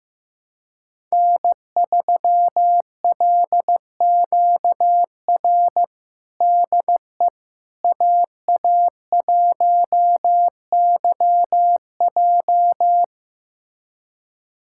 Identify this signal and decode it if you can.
Morse code, heard as a keyed tone, about 15 wpm, 700 Hz. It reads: N3LQR DE AA1YJ
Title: USA Callsign DE Callsign - 5 Characters - 15wpm